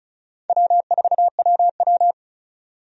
Morse code, heard as a keyed tone, about 36 wpm, 700 Hz. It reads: W4WW
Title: USA Call Signs - 4 Characters - 35wpm